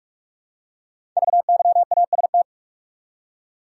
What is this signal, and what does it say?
Morse code, heard as a keyed tone, about 44 wpm, 690 Hz. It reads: VYAST
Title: Code Groups - 5 Letters - 45wpm